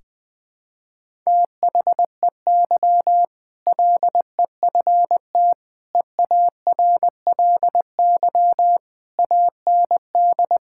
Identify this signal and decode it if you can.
Morse code, heard as a keyed tone, about 20 wpm, 700 Hz. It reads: THEY LEFT EARLY AND
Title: Sentences with 2-Word Phrases - Spoken Beginning - 20wpm